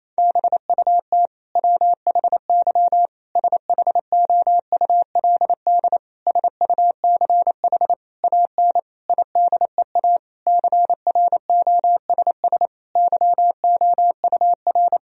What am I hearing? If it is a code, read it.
Morse code, heard as a keyed tone, about 28 wpm, 700 Hz. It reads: BUT WHY SHOULD SUCH AN IDEA CROSS YOUR
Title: Sentences from Top 1000 Words - Review Part 1 - 28wpm